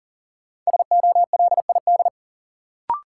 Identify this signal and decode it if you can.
Morse code, heard as a keyed tone, about 40 wpm, 700 Hz. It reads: SOLID
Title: Sentences from Top 1500 Words - New Word Drill - 40wpm